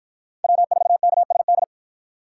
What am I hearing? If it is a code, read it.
Morse code, heard as a keyed tone, about 48 wpm, 700 Hz. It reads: W4KSD